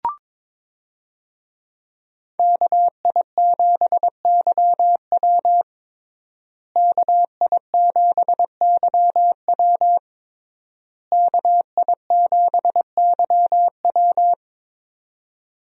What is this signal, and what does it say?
Morse code, heard as a keyed tone, about 22 wpm, 700 Hz. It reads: KI7YW KI7YW KI7YW
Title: USA Call Signs - 5 Characters - Repeated 3x - 22wpm